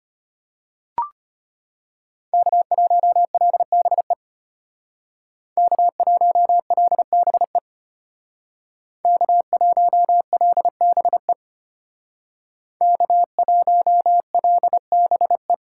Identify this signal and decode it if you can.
Morse code, heard as a keyed tone, about 30 wpm, 700 Hz. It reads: K1LBE K1LBE K1LBE K1LBE
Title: Speed Racing - USA Call Signs 25wpm